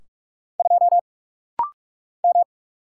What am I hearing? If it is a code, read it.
Morse code, heard as a keyed tone, about 46 wpm, 700 Hz. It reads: 2 M
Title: Single Letter-Number - Rapid-Fire 45wpm